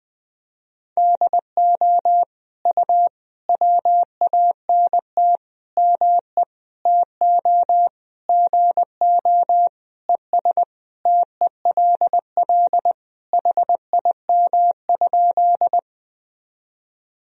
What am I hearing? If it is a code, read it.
Morse code, heard as a keyed tone, about 20 wpm, 700 Hz. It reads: DO U WANT ME TO GO ES TELL HIM?
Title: Sentences from Top 200 Words and Abbreviations 20wpm